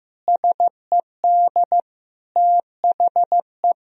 Morse code, heard as a keyed tone, about 14 wpm, 700 Hz. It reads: SED THE
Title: Sentences from Top 400 Words and Abbreviations 15wpm